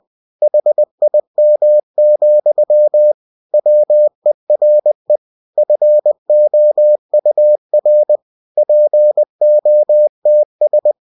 Morse code, heard as a keyed tone, about 20 wpm, 600 Hz. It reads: HIM, WERE FOUR POTS
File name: cw-books / winnie-the-pooh / winnie-20wpm-page-0094